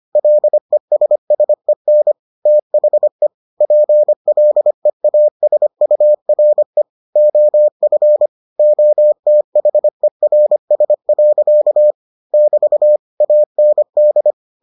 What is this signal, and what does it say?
Morse code, heard as a keyed tone, about 25 wpm, 600 Hz. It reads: LESSEN THE PLEASURE OF OTHERS. <BT> AND